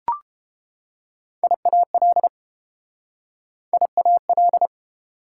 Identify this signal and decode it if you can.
Morse code, heard as a keyed tone, about 31 wpm, 700 Hz. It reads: IAL IAL